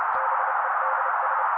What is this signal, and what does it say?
Morse code, heard as a keyed tone, about 30 wpm, 530 Hz. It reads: BEDS
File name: daily / 29wpm noise